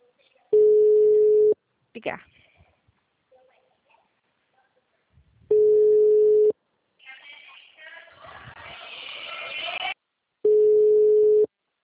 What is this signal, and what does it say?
Ringback tone (Europe, 425 Hz), with speech.